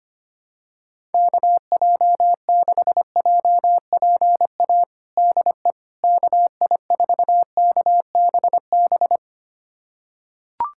Morse code, heard as a keyed tone, about 25 wpm, 700 Hz. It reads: KJ6JPA DE KI4KBB
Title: USA Callsign DE Callsign - 6 Characters - 25wpm